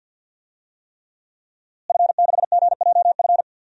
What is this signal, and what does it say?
Morse code, heard as a keyed tone, about 53 wpm, 700 Hz. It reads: F6GJF
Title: DXCC-100 Call Signs - 5 Characters - Repeated 3x - 50wpm